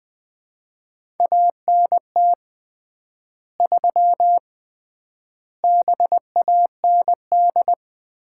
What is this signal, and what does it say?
Morse code, heard as a keyed tone, about 20 wpm, 700 Hz. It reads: ANT 3 BAND
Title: Common QSO Phrases - 3x Word-Spacing - 20wpm